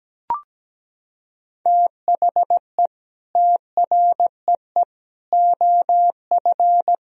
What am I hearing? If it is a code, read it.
Morse code, heard as a keyed tone, about 17 wpm, 700 Hz. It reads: THE TREE OF
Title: Sentences from Top 1000 Words - Review Part 3 - 17wpm